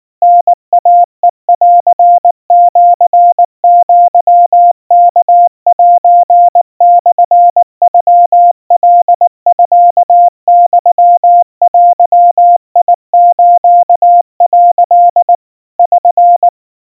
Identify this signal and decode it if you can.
Morse code (Japanese Wabun code), heard as a keyed tone, about 19 wpm, 700 Hz. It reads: タイヘンシアワセモノカミユデス」 <SN>
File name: kawa / Morse sample / kamyu 70 cpm 700 Hz Crispy